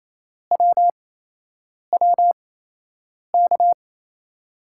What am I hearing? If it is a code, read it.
Morse code, heard as a keyed tone, about 28 wpm, spaced as for 8 wpm, 700 Hz. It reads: WWK
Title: USA Call Signs - 5 Characters - 8x Character Spacing - 28wpm